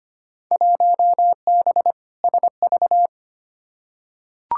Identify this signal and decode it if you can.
Morse code, heard as a keyed tone, about 25 wpm, 700 Hz. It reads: 1B SV
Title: Field Day Exchanges Encore - 25wpm